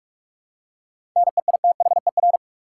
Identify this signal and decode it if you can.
Morse code, heard as a keyed tone, about 46 wpm, 700 Hz. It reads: NEITHER
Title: Seven Letter Words - 45wpm